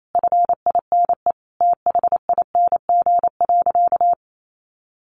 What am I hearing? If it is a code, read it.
Morse code, heard as a keyed tone, about 28 wpm, 700 Hz. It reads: FINE THING.
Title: Sentences from Top 500 Words 28wpm